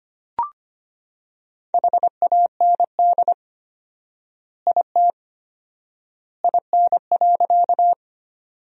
Morse code, heard as a keyed tone, about 25 wpm, 700 Hz. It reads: HAND IT IN.
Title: 4x Word-Spacing - Sentences from Top 200 Words - 25wpm